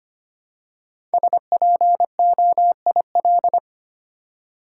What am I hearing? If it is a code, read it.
Morse code, heard as a keyed tone, about 25 wpm, 700 Hz. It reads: SPOIL
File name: 5 Letter Words - First 3 Letters Given - 25wpm